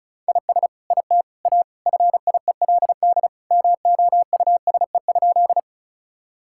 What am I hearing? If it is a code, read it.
Morse code, heard as a keyed tone, about 36 wpm, 690 Hz. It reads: IS IT A FIELD MOUSE?
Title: Sentences from Top 1000 Words - Review Part 5 - 35wpm